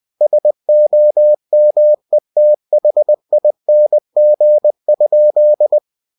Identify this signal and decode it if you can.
Morse code, heard as a keyed tone, about 20 wpm, 600 Hz. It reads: SOMETHING?
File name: cw-books / winnie-the-pooh / winnie-20wpm-page-0065